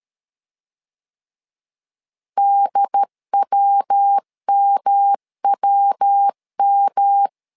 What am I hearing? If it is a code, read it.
Morse code, heard as a keyed tone, about 13 wpm, 800 Hz. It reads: DWMWM
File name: LEZIONI / 3 MWRO / 3 MW 12wpm 1633